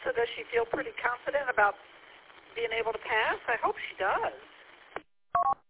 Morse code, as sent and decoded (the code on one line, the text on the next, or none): none